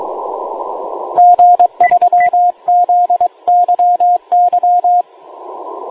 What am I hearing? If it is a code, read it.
Morse code, heard as a keyed tone, about 22 wpm, 710 Hz, with radio noise. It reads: G3ZYY